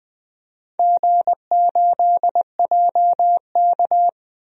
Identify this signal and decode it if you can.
Morse code, heard as a keyed tone, about 20 wpm, 700 Hz. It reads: G8JK
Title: DXCC-100 Call Signs - 4 Characters - 20wpm